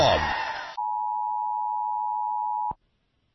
Emergency Alert System (SAME) tones (the 853 and 960 Hz attention tone). It is heard from 0.7 to 2.8 s.